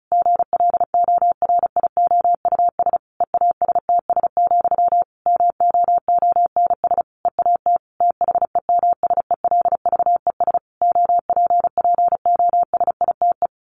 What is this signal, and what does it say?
Morse code, heard as a keyed tone, about 34 wpm, 700 Hz. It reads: GLORIOUS EASTS, MOONS EAT THEMSELVES OPPOSITE
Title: A Sound Of Thunder 35wpm